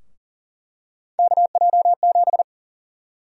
Morse code, heard as a keyed tone, about 40 wpm, 700 Hz. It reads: KJ7
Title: USA Call Sign Prefixes - Repeated 2x - 40wpm